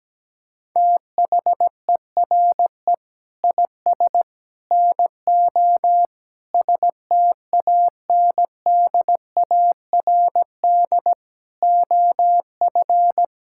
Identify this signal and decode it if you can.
Morse code, heard as a keyed tone, about 17 wpm, 700 Hz. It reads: THERE IS NO STANDARD OF
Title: Sentences from Top 1600 Words 17wpm